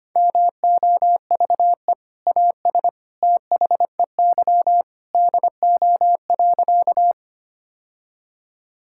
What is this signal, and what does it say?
Morse code, heard as a keyed tone, about 25 wpm, 700 Hz. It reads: MOVE AS THEY DO.
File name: practice / 25wpm / Sentences with 2-Word Phrases - 25wpm